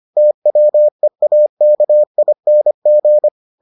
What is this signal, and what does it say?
Morse code, heard as a keyed tone, about 25 wpm, 600 Hz. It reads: TWEAKING